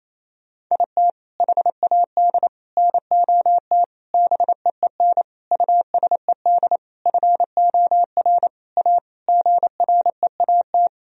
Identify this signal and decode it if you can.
Morse code, heard as a keyed tone, about 28 wpm, 700 Hz. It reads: IT HAD NOT BEEN USED FOR A GREAT
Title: Sentences from Top 1000 Words - Review Part 1 - 28wpm